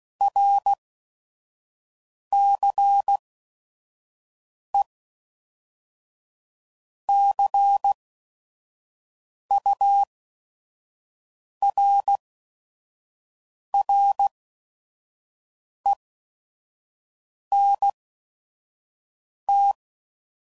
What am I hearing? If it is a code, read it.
Morse code, heard as a keyed tone, about 16 wpm, spaced as for 5 wpm, 800 Hz. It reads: RCE CURRENT
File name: CW Code Groups @5WPM / Technical Subjects / XC